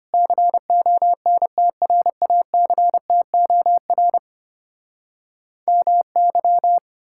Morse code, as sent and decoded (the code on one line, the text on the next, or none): -.-. --- -. - .-. .- -.-. - --- .-. / -- -.--
CONTRACTOR MY